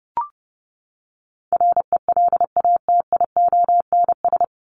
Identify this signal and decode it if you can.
Morse code, heard as a keyed tone, about 30 wpm, 700 Hz. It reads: RELATIONS
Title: ES:S Suffix - Top 100 Words with Base Words 30wpm